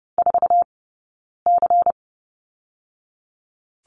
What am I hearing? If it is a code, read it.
Morse code, heard as a keyed tone, about 30 wpm, 700 Hz. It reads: 4 C
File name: Lesson 18d - Letter m - Two Characters 30wpm